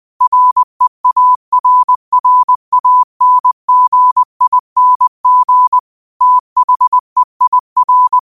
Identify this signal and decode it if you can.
Morse code, heard as a keyed tone, about 20 wpm, 1000 Hz. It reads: REARRANGING THEIR
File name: qotdfiles / 20wpm